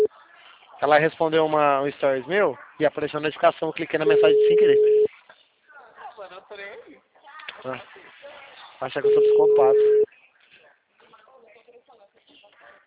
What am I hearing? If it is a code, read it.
Ringback tone (Europe, 425 Hz), with speech.